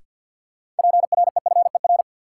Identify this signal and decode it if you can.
Morse code, heard as a keyed tone, about 52 wpm, 700 Hz. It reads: PREFER
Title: Sentences from Top 1100 Words - New Word Drill - 50wpm